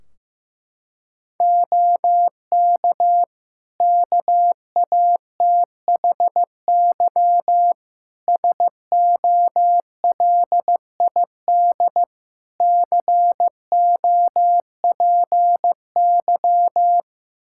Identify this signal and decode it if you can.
Morse code, heard as a keyed tone, about 16 wpm, 700 Hz. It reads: OK KATHY SOLID COPY
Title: Practice QSO - Exchange 3 - More Info - 15wpm